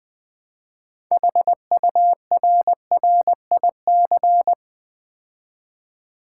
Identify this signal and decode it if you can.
Morse code, heard as a keyed tone, about 20 wpm, 700 Hz. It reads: HURRIC
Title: Build A Word - 3 Letters At A Time - 9-Letter Words - 20wpm